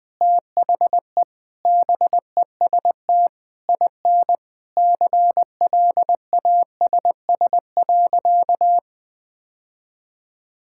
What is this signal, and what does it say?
Morse code, heard as a keyed tone, about 21 wpm, 700 Hz. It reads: THE BEST IN CLASS.
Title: Sentences with 8-Word Phrases - 20wpm